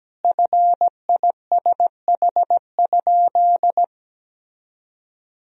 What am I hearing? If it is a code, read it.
Morse code, heard as a keyed tone, about 17 wpm, 700 Hz. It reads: FISH?
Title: Sentences from Top 1000 Words - Review Part 1 - 17wpm